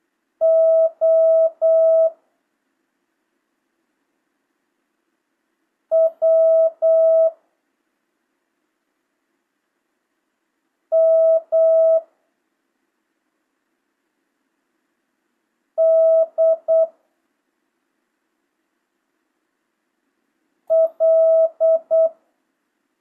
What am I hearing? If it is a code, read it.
Morse code, heard as a keyed tone, about 8 wpm, 650 Hz. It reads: O W M D L
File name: jh8cbh / 2025cw